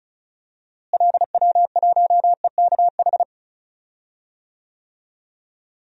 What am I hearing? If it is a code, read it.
Morse code, heard as a keyed tone, about 36 wpm, 700 Hz. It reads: LW1EKH
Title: DXCC-100 Call Signs - 6 Characters - Repeated 2x - 35wpm